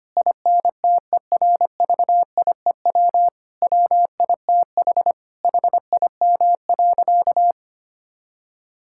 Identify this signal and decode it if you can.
Morse code, heard as a keyed tone, about 25 wpm, 700 Hz. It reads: INTERVIEW WITH HIM.
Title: Sentences from Top 1600 Words 25wpm